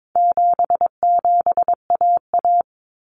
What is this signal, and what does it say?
Morse code, heard as a keyed tone, about 22 wpm, 700 Hz. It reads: 77AA
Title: Common Cut Numbers - 4-Digit 22wpm